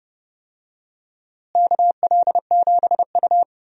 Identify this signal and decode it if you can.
Morse code, heard as a keyed tone, about 30 wpm, 700 Hz. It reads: KL7U
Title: DXCC-100 Call Signs - 4 Characters - Repeated 2x - 30wpm